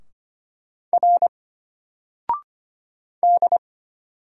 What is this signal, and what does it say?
Morse code, heard as a keyed tone, about 25 wpm, 700 Hz. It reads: R D
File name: Single-Letter-Number-25wpm